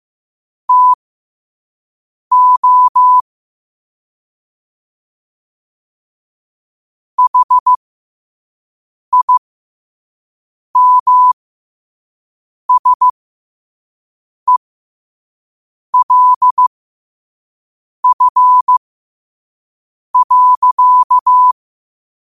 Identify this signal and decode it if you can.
Morse code, heard as a keyed tone, about 15 wpm, spaced as for 5 wpm, 1000 Hz. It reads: TO HIMSELF.
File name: qotdfiles / 5wpm